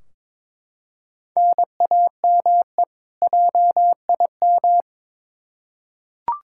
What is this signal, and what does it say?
Morse code, heard as a keyed tone, about 22 wpm, 700 Hz. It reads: NAME JIM
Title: K1USN SST QSO Practice 22wpm